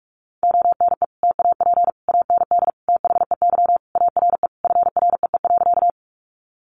Morse code, heard as a keyed tone, about 44 wpm, 700 Hz. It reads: ONE TAP AND THEY ARE FREE.
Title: Sentences from Top 1400 Words 45wpm